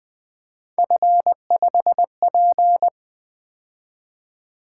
Morse code, heard as a keyed tone, about 20 wpm, 700 Hz. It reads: F5P
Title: Three Letter-Number Combinations - 20wpm